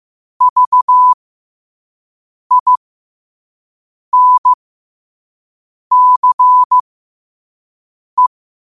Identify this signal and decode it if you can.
Morse code, heard as a keyed tone, about 15 wpm, spaced as for 5 wpm, 1000 Hz. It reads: VINCE